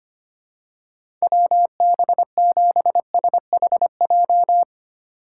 Morse code, heard as a keyed tone, about 25 wpm, 700 Hz. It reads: WB7SHJ
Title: USA Call Signs - 6 Characters - 2x Character Spacing - 25wpm